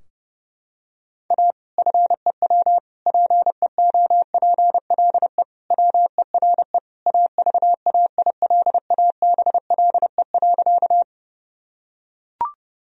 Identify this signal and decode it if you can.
Morse code, heard as a keyed tone, about 30 wpm, 700 Hz. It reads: A FEW PEOPLE WERE AVAILABLE.
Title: Sentences from Top 1000 Words - Review Part 3 - 30wpm